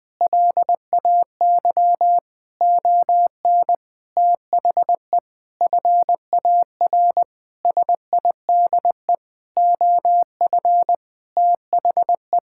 Morse code, heard as a keyed tone, about 20 wpm, 700 Hz. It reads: LAY ON THE FAR SIDE OF THE
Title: Sentences with 5-Word Phrases - 20wpm